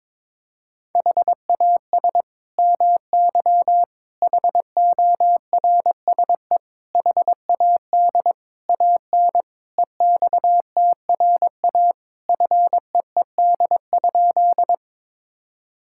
Morse code, heard as a keyed tone, about 22 wpm, 700 Hz. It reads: HAS MY HORSE HAD AN EXTRA FEED?
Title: Sentences from Top 1600 Words 22wpm